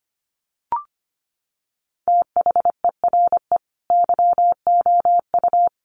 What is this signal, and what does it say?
Morse code, heard as a keyed tone, about 25 wpm, 700 Hz. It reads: THERE YOU
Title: Sentences from Top 100 Words 25wpm